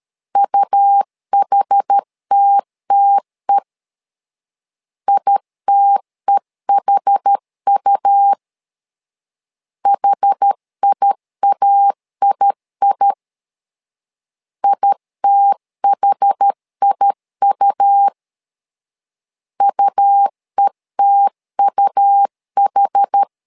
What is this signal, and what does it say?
Morse code, heard as a keyed tone, about 13 wpm, 800 Hz. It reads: UHTTE ITEHU HIAII ITHIU UETUH
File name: LEZIONI / 2 UHDN / 2 UH 12wpm 1439